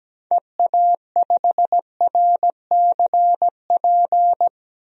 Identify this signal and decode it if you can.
Morse code, heard as a keyed tone, about 17 wpm, 700 Hz. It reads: EA5RCP
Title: DXCC-100 Call Signs - 6 Characters - Repeated 3x - 17wpm